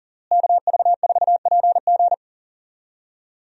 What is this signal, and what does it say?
Morse code, heard as a keyed tone, about 40 wpm, 700 Hz. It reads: KV4PG
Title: USA Call Signs - 5 Characters - Repeated 3x - 40wpm